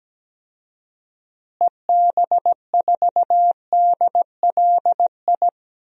Morse code, heard as a keyed tone, about 17 wpm, 700 Hz. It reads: EB4DLI
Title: DXCC-100 Call Signs - 6 Characters - Repeated 2x - 17wpm